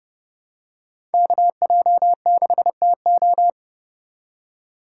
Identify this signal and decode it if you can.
Morse code, heard as a keyed tone, about 30 wpm, 700 Hz. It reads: KJ6TO
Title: USA Call Signs - 5 Characters - 3x Character Spacing - 30wpm